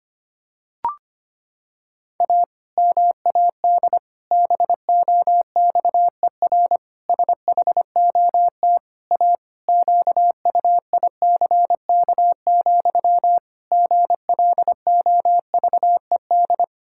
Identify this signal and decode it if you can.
Morse code, heard as a keyed tone, about 25 wpm, 700 Hz. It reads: A MAD BOXER SHOT A QUICK, GLOVED
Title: Pangram Challenge v2 - 25wpm